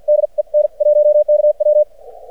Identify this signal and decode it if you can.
Morse code, heard as a keyed tone, about 56 wpm, spaced as for 44 wpm, 600 Hz. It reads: DE N1KW
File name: archived / CRSnet